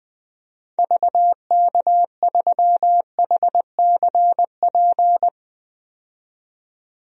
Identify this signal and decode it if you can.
Morse code, heard as a keyed tone, about 20 wpm, 700 Hz. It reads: VK3HCP
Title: DXCC-100 Call Signs - 6 Characters - 20wpm